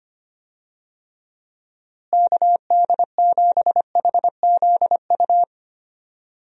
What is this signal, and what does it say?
Morse code, heard as a keyed tone, about 25 wpm, 700 Hz. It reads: KD7HZU
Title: USA Call Signs - 6 Characters - Repeated 3x - 25wpm